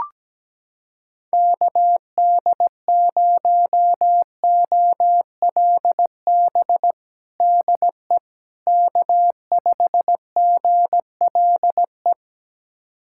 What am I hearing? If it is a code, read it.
Morse code, heard as a keyed tone, about 17 wpm, 700 Hz. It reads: KD0OLB DE K5GLE
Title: USA Callsign DE Callsign 17wpm